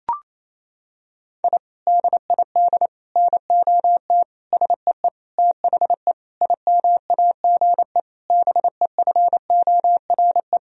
Morse code, heard as a keyed tone, about 28 wpm, 700 Hz. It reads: I DID NOT SEE THE IMAGE BEFORE
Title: Sentences from Top 1000 Words - Review Part 4 - 28wpm